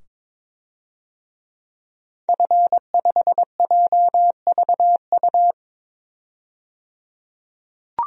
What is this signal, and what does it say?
Morse code, heard as a keyed tone, about 22 wpm, 700 Hz. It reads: F5JVU
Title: DXCC-100 Call Signs - 5 Characters - 22wpm